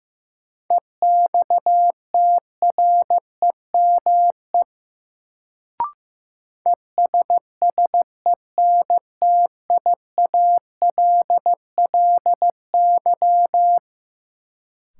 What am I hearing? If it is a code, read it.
Morse code, heard as a keyed tone, about 15 wpm, 700 Hz. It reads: EXTREME ESSENTIALLY
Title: Sentences from Top 1300 Words - New Word Drill - 15wpm